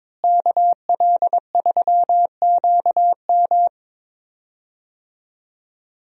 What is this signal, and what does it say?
Morse code, heard as a keyed tone, about 22 wpm, 700 Hz. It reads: KL3QM
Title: USA Call Signs - 5 Characters - Repeated 2x - 22wpm